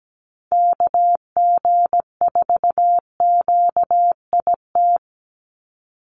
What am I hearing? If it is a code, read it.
Morse code, heard as a keyed tone, about 17 wpm, 700 Hz. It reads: KG4QIT
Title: USA Call Signs 17wpm